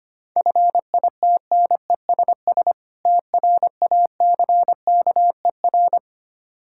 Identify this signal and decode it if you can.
Morse code, heard as a keyed tone, about 25 wpm, 700 Hz. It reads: FITNESS TRACKER
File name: practice / 25wpm / Difficult Words - Speed-Racing with Hints - 25wpm